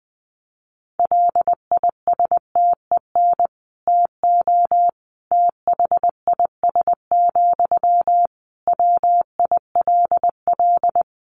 Morse code, heard as a keyed tone, about 19 wpm, 700 Hz. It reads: LISTEN TO THIS, WILL
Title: Sentences from Top 400 Words 20wpm